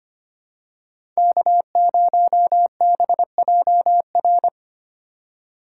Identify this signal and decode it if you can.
Morse code, heard as a keyed tone, about 25 wpm, 700 Hz. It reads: K0BJR
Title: USA Call Signs - 4x Character Spacing - 25wpm